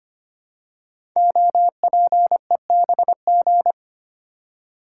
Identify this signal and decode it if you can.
Morse code, heard as a keyed tone, about 25 wpm, 700 Hz. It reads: OPEBG
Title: Code Groups - 5 Letter:Numbers - 25wpm